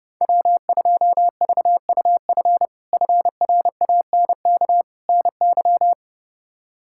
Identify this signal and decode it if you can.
Morse code, heard as a keyed tone, about 30 wpm, 700 Hz. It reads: W2VUF FRANK NY
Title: K1USN SST QSO Practice - 30wpm